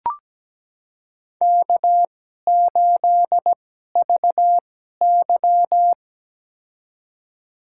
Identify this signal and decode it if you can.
Morse code, heard as a keyed tone, about 17 wpm, spaced as for 12 wpm, 700 Hz. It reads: K8VY
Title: USA Call Signs - 4 Characters - 2x Character Spacing - 17wpm